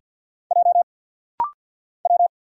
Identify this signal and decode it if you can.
Morse code, heard as a keyed tone, about 50 wpm, 700 Hz. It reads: J W
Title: Single Letter-Number Alt - Rapid-Fire 50wpm